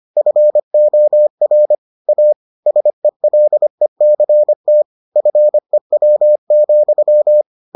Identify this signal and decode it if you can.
Morse code, heard as a keyed tone, about 25 wpm, 600 Hz. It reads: FOR A SELECT FEW,